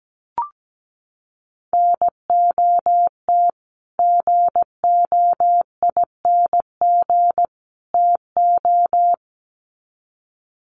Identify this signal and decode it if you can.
Morse code, heard as a keyed tone, about 17 wpm, 700 Hz. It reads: NOT GOING TO
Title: Sets of 3 Words 17wpm